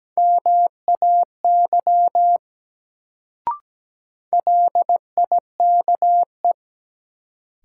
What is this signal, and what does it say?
Morse code, heard as a keyed tone, about 17 wpm, 700 Hz. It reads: MAY LIKE
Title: Sentences from Top 100 Words - New Word Drill - 17wpm